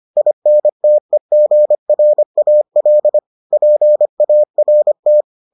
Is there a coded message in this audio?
Yes: Morse code from 0.2 to 5.2 s, heard as a keyed tone, about 25 wpm, 600 Hz. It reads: INTEGRAL PART